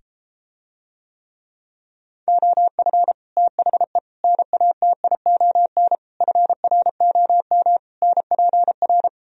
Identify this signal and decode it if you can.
Morse code, heard as a keyed tone, about 33 wpm, 700 Hz. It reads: OF THE NATION FROM NPR